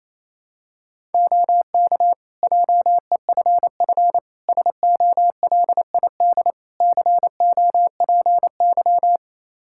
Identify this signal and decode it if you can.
Morse code, heard as a keyed tone, about 28 wpm, 700 Hz. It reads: OK JEFF SOLID COPY
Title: K1USN SST QSO Practice 28wpm